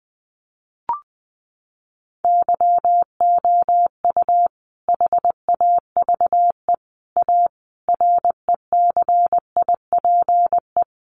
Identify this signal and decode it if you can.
Morse code, heard as a keyed tone, about 20 wpm, 700 Hz. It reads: YOU HAVE A RECIPE
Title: Sentences from Top 1200 Words 20wpm